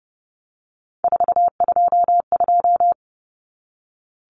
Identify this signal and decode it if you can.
Morse code, heard as a keyed tone, about 30 wpm, 700 Hz. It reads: 422
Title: Numbers - 3-Digit 30wpm